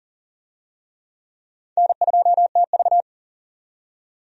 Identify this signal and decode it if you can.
Morse code, heard as a keyed tone, about 40 wpm, 700 Hz. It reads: N2TV
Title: USA Call Signs - 4 Characters - 40wpm